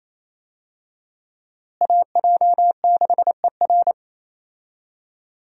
Morse code, heard as a keyed tone, about 28 wpm, 700 Hz. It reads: AJ6ER
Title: USA Call Signs - 5 Characters - Repeated 3x - 28wpm